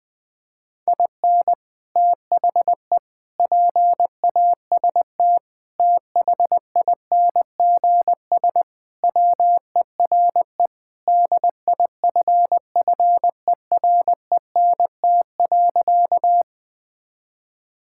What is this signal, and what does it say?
Morse code, heard as a keyed tone, about 20 wpm, 700 Hz. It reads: IN THE PAST THINGS WERE DIFFERENT.
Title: Sentences with 3-Word Phrases - 20wpm